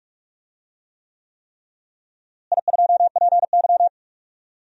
Morse code, heard as a keyed tone, about 46 wpm, 700 Hz. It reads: I2PY
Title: DXCC-100 Call Signs - 4 Characters - Repeated 3x - 45wpm